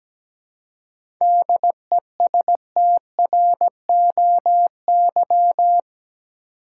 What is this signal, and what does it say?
Morse code, heard as a keyed tone, about 17 wpm, 700 Hz. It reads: DESTROY
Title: Seven Letter Words - 17wpm